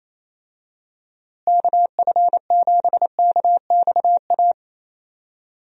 Morse code, heard as a keyed tone, about 28 wpm, 700 Hz. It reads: KF7KXA
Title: USA Call Signs - 6 Characters - Repeated 3x - 28wpm